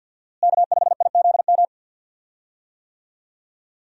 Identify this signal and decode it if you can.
Morse code, heard as a keyed tone, about 52 wpm, 700 Hz. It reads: K5IZM
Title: USA Call Signs - 5 Characters - Repeated 3x - 50wpm